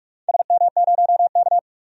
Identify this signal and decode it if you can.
Morse code, heard as a keyed tone, about 46 wpm, 700 Hz. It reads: SM0K
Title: DXCC-100 Call Signs - 4 Characters - 45wpm